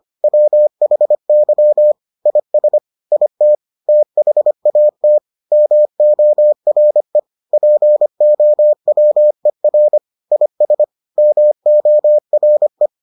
Morse code, heard as a keyed tone, about 25 wpm, 600 Hz. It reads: WHY IS IT THAT MORE POWER IS MORE